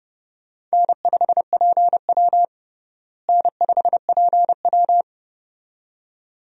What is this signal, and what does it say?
Morse code, heard as a keyed tone, about 32 wpm, 700 Hz. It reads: N5PW N5PW
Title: USA Call Signs - 4 Characters - Repeated 2x - 30wpm